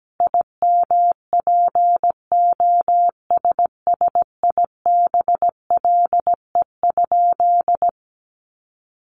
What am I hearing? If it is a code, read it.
Morse code, heard as a keyed tone, about 17 wpm, 700 Hz. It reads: IMPOSSIBLE?
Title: Sentences from Top 1100 Words 17wpm